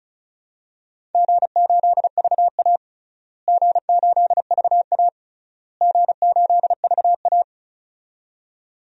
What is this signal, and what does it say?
Morse code, heard as a keyed tone, about 36 wpm, 700 Hz. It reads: G8VA G8VA G8VA
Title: DXCC-100 Call Signs - 4 Characters - Repeated 3x - 35wpm